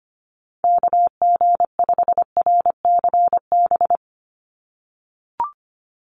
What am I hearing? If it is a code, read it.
Morse code, heard as a keyed tone, about 25 wpm, 700 Hz. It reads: KG5RCB
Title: USA Call Signs 25wpm